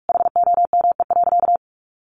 Morse code, heard as a keyed tone, about 44 wpm, 700 Hz. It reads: HOME.
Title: Sentences from Top 1400 Words 45wpm